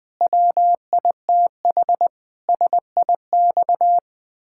Morse code, heard as a keyed tone, about 20 wpm, 700 Hz. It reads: WITH SIX
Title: Pangram Challenge v2 - 20wpm